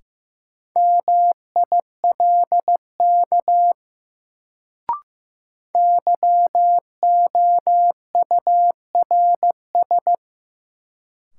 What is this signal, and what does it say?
Morse code, heard as a keyed tone, about 15 wpm, 700 Hz. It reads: MILK YOURS
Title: Sentences from Top 1500 Words - New Word Drill - 15wpm